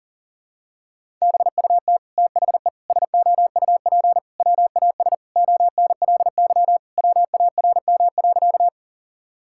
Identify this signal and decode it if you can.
Morse code, heard as a keyed tone, about 40 wpm, 700 Hz. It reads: BUT THE SOUP WAS ONLY WARM.
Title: Sentences from Top 1000 Words - Review Part 6 - 40wpm